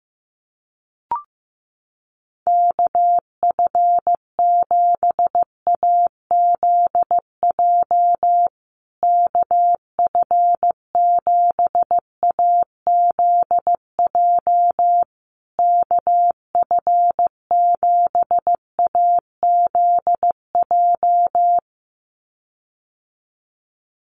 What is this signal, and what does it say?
Morse code, heard as a keyed tone, about 15 wpm, 700 Hz. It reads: KF7AZJ KF7AZJ KF7AZJ
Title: USA Call Signs Repeated 3x 15wpm